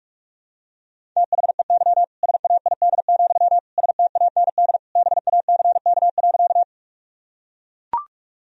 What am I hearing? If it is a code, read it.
Morse code, heard as a keyed tone, about 42 wpm, 700 Hz. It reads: THEY SAID, STAND BACK.
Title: Sentences from Top 1000 Words - Review Part 1 - 45wpm